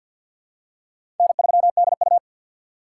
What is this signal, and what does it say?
Morse code, heard as a keyed tone, about 52 wpm, 700 Hz. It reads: N3DU
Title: USA Call Signs - 4 Characters - 3x Character Spacing - 50wpm